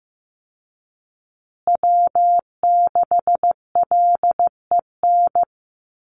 Morse code, heard as a keyed tone, about 15 wpm, 700 Hz. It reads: W6LEN
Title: Call Signs Repeated 3x - 15wpm